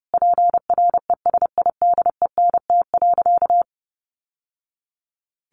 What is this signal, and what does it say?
Morse code, heard as a keyed tone, about 30 wpm, 700 Hz. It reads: PRESIDENT.
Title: Sentences from Top 1000 Words 30wpm